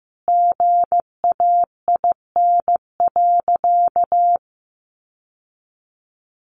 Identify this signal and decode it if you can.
Morse code, heard as a keyed tone, about 15 wpm, 700 Hz. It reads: GAIN.
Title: Sentences from Top 1100 Words 15wpm